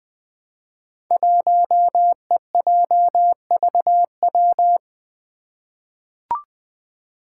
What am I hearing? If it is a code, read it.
Morse code, heard as a keyed tone, about 20 wpm, 700 Hz. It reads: 1EJVW
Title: Five Challenging Letter-Number Combinations 20wpm